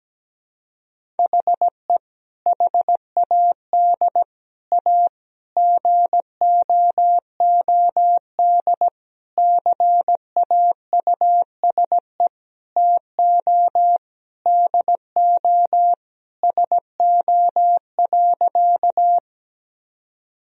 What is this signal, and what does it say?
Morse code, heard as a keyed tone, about 17 wpm, 700 Hz. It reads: HE HAD A GOOD CAUSE TO DO SO.
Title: Sentences from Top 1000 Words - Review Part 1 - 17wpm